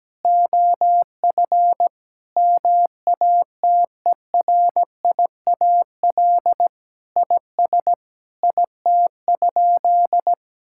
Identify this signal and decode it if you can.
Morse code, heard as a keyed tone, about 17 wpm, 700 Hz. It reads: OF MATERIAL IS IT?
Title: Sentences from Top 1000 Words - Review Part 5 - 17wpm